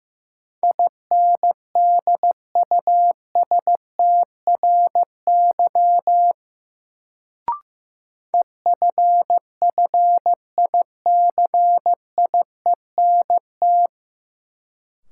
Morse code, heard as a keyed tone, about 15 wpm, 700 Hz. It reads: INDUSTRY EFFICIENT
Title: Sentences from Top 1300 Words - New Word Drill Alt - 15wpm